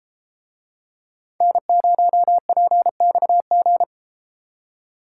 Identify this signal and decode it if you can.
Morse code, heard as a keyed tone, about 31 wpm, 700 Hz. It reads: N0PXG